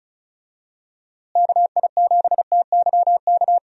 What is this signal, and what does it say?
Morse code, heard as a keyed tone, about 36 wpm, 700 Hz. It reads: KI7TYK
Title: USA Call Signs - 6 Characters - Repeated 2x - 35wpm